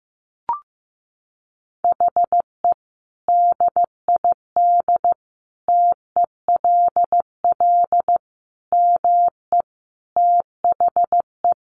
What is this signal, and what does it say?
Morse code, heard as a keyed tone, about 15 wpm, 700 Hz. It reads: HE DID TELL ME THE
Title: Sentences from Top 1000 Words 15wpm